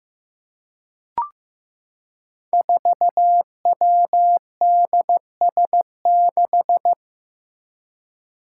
Morse code, heard as a keyed tone, about 15 wpm, 700 Hz. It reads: 4WDS6
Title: Five Challenging Letter-Number Combinations 15wpm